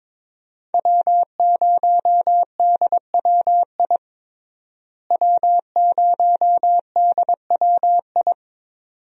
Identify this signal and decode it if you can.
Morse code, heard as a keyed tone, about 22 wpm, 700 Hz. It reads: W0DWI W0DWI
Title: USA Call Signs - 5 Characters - Repeated 2x - 22wpm